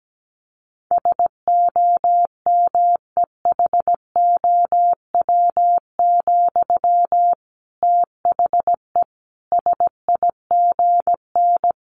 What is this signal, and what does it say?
Morse code, heard as a keyed tone, about 17 wpm, 700 Hz. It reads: SOMEHOW, THE SIGN
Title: A Sound Of Thunder 17wpm